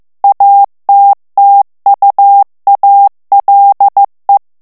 Morse code, heard as a keyed tone, about 15 wpm, 800 Hz. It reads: ATTUALE
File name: ITALIEN texte clair 1 vitesse 14 mots minute